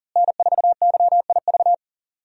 Morse code, heard as a keyed tone, about 37 wpm, 700 Hz. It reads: N4YIV
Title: USA Call Signs - 5 Characters - Repeated 2x - 40wpm